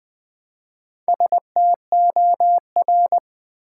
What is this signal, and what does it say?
Morse code, heard as a keyed tone, about 20 wpm, 700 Hz. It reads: STOR
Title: Build A Word - 1 Letter At A Time - 5-Letter Words - 20wpm